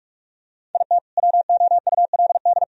Morse code, heard as a keyed tone, about 46 wpm, 700 Hz. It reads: IT WOULD
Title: Sentences from Top 1000 Words - Review Part 1 - 45wpm